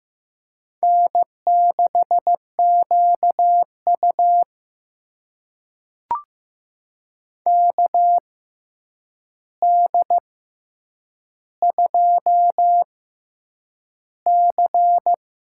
Morse code, heard as a keyed tone, about 15 wpm, spaced as for 8 wpm, 700 Hz. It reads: N6QU KD2C
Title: USA Call Signs - 4 Characters - 6x Character Spacing - 15wpm